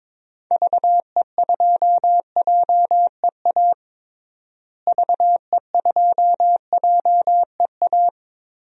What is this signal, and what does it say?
Morse code, heard as a keyed tone, about 21 wpm, 700 Hz. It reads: VE2JEA VE2JEA
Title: DXCC-100 Call Signs - 6 Characters - Repeated 2x - 22wpm